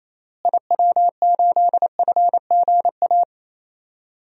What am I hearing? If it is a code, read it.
Morse code, heard as a keyed tone, about 28 wpm, 700 Hz. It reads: IW8FGA
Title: DXCC-100 Call Signs - 6 Characters - Repeated 3x - 28wpm